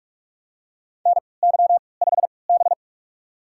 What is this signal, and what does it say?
Morse code, heard as a keyed tone, about 42 wpm, spaced as for 25 wpm, 700 Hz. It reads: NY5B
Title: USA Call Signs - 4 Characters - 3x Character Spacing - 45wpm